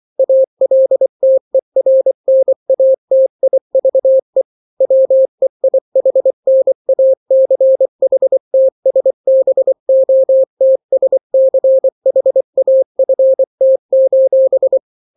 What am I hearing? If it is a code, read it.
Morse code, heard as a keyed tone, about 24 wpm, 530 Hz. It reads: ALTERNATIVE WEIHNACHTSBOTSCHAFT:
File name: daily / 23wpm